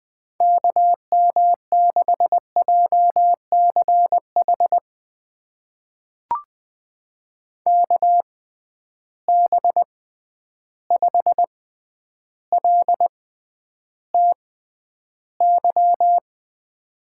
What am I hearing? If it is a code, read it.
Morse code, heard as a keyed tone, about 20 wpm, spaced as for 10 wpm, 700 Hz. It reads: KM6JCH KB5LTY
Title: USA Call Signs - 6 Characters - 6x Character Spacing - 20wpm